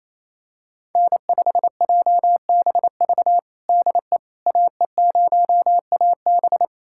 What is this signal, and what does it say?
Morse code, heard as a keyed tone, about 28 wpm, 700 Hz. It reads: N5JBV DE AE0AB
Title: USA Callsign DE Callsign - 5 Characters - 28wpm